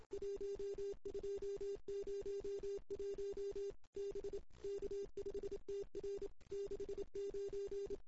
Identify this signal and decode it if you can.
Morse code, heard as a keyed tone, about 24 wpm, 410 Hz. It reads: 1201 B K5TR 69
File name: CW / 2008-ss-cw-k5tr-op-n5rz -17:00